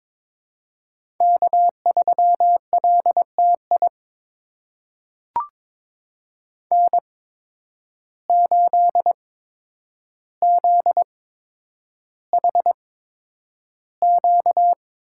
Morse code, heard as a keyed tone, about 22 wpm, spaced as for 9 wpm, 700 Hz. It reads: K3LTI N8ZHQ
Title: USA Call Signs - 5 Characters - 8x Character Spacing - 22wpm